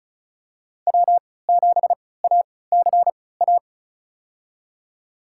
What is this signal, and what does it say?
Morse code, heard as a keyed tone, about 36 wpm, spaced as for 20 wpm, 690 Hz. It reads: W7ACA
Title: USA Call Signs - 5 Characters - 3x Character Spacing - 35wpm